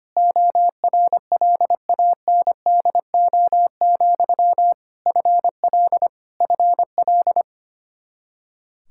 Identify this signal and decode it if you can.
Morse code, heard as a keyed tone, about 25 wpm, 700 Hz. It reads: ORLANDO, FL FL
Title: Practice QSO - Exchange 1 - Beginning - 25wpm